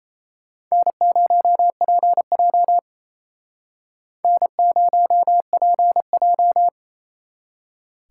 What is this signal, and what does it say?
Morse code, heard as a keyed tone, about 33 wpm, 700 Hz. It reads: N0PJ N0PJ
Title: Speed Racing - USA 1x2 Call Signs 28wpm